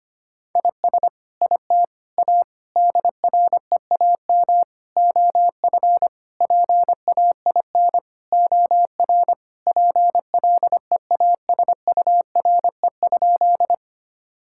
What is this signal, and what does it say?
Morse code, heard as a keyed tone, about 25 wpm, 700 Hz. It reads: IS IT A DREAM OF PAIN OR PLEASURE?
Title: Sentences from Top 1600 Words 25wpm